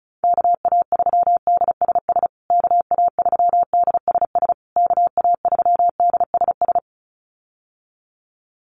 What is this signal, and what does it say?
Morse code, heard as a keyed tone, about 34 wpm, 700 Hz. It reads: KA3DSS KA3DSS KA3DSS
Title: USA Call Signs Repeated 3x 35wpm